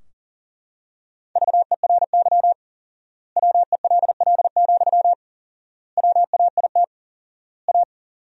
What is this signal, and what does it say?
Morse code, heard as a keyed tone, about 40 wpm, 700 Hz. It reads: VERY WELL, WAIT A
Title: 4x Word-Spacing - Sentences from Top 500 Words - 40wpm